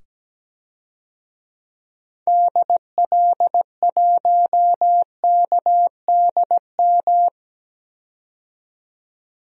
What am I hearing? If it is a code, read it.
Morse code, heard as a keyed tone, about 17 wpm, 700 Hz. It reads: DL1KDM
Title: DXCC-100 Call Signs - 6 Characters - 17wpm